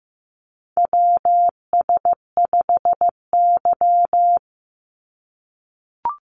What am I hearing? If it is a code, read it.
Morse code, heard as a keyed tone, about 15 wpm, 700 Hz. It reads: WS5Y
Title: USA Call Signs 15wpm